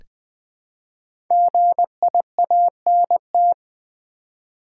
Morse code, heard as a keyed tone, about 20 wpm, 700 Hz. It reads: GIANT
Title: Build A Word - 1 Letter At A Time - 3 to 5-Letter Words - 20wpm